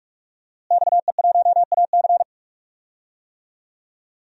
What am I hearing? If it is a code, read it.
Morse code, heard as a keyed tone, about 46 wpm, 700 Hz. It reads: XE1AC
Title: DXCC-100 Call Signs - 5 Characters - Repeated 3x - 45wpm